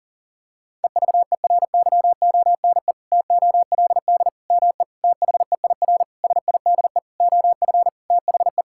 Morse code, heard as a keyed tone, about 40 wpm, 700 Hz. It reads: EVERYONE TOLD ME THEIR SIDE OF THE
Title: Sentences from Top 1000 Words - Review Part 5 - 40wpm